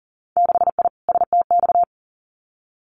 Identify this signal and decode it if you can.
Morse code, heard as a keyed tone, about 41 wpm, 700 Hz. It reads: 6I STX
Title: Winter Field Day Exchanges Encore 40wpm